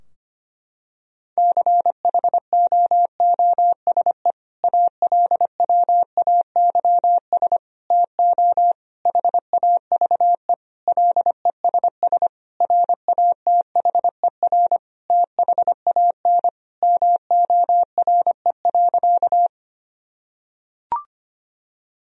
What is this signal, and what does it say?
Morse code, heard as a keyed tone, about 25 wpm, 700 Hz. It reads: CHOOSE ALWAYS TO HAVE LESS RATHER THAN MORE.
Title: Sentences from Top 1000 Words - Review Part 4 - 25wpm